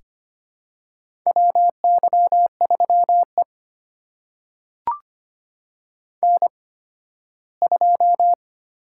Morse code, heard as a keyed tone, about 25 wpm, spaced as for 15 wpm, 700 Hz. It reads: WY3E N2
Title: USA Call Signs - 4 Characters - 8x Character Spacing - 25wpm